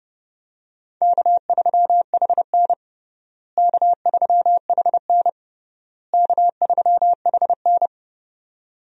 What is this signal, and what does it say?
Morse code, heard as a keyed tone, about 30 wpm, 700 Hz. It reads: K3HN K3HN K3HN
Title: USA Call Signs - 4 Characters - Repeated 3x - 30wpm